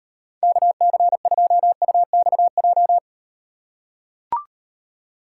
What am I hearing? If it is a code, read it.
Morse code, heard as a keyed tone, about 38 wpm, 700 Hz. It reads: KC2UXJ